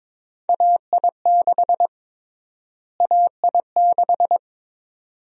Morse code, heard as a keyed tone, about 22 wpm, 700 Hz. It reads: AI6 AI6
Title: USA Call Sign Prefixes - Repeated 2x - 22wpm